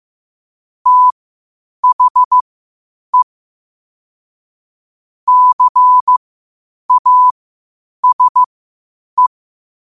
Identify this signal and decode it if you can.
Morse code, heard as a keyed tone, about 15 wpm, spaced as for 8 wpm, 1000 Hz. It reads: THE CASE
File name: qotdfiles / 8wpm